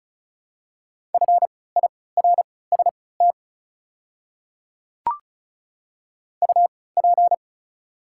Morse code, heard as a keyed tone, about 36 wpm, spaced as for 20 wpm, 690 Hz. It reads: FIRST UP
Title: 3x Character Spacing - Top 100 Words - 35wpm